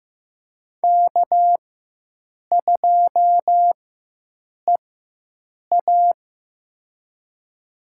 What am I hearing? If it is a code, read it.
Morse code, heard as a keyed tone, about 15 wpm, spaced as for 7 wpm, 700 Hz. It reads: K2EA
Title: USA Call Signs - 4 Characters - 4x Character Spacing - 15wpm